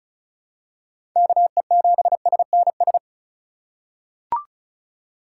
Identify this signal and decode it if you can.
Morse code, heard as a keyed tone, about 34 wpm, 700 Hz. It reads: KE7SNS
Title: USA Call Signs - 6 Characters - 4x Character Spacing - 35wpm